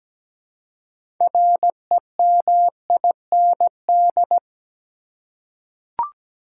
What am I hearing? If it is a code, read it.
Morse code, heard as a keyed tone, about 17 wpm, 700 Hz. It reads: REMIND
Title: Six Letter Words - 17wpm